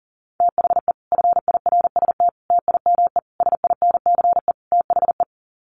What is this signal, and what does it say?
Morse code, heard as a keyed tone, about 40 wpm, 700 Hz. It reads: THE FIRST TIME SINCE THE
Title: Sets of 5 Words Encore 40wpm